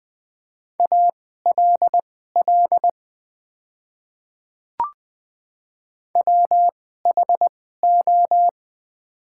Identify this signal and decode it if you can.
Morse code, heard as a keyed tone, about 20 wpm, spaced as for 15 wpm, 700 Hz. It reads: ALL WHO
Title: 2x Character Spacing - Top 100 Words - 20wpm